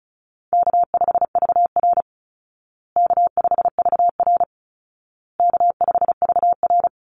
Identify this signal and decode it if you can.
Morse code, heard as a keyed tone, about 34 wpm, 700 Hz. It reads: K5VR K5VR K5VR
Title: Easy Call Signs Repeated 3x - 35wpm